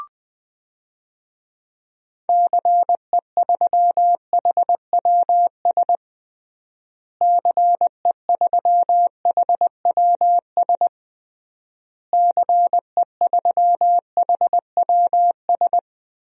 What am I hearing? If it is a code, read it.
Morse code, heard as a keyed tone, about 20 wpm, 700 Hz. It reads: CE3HWS CE3HWS CE3HWS
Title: DXCC-100 Call Signs - 6 Characters - Repeated 3x - 20wpm